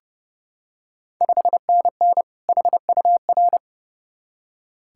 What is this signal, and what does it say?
Morse code, heard as a keyed tone, about 30 wpm, 700 Hz. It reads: 5NN HUR
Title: Ontario QSO Party Exchange 30wpm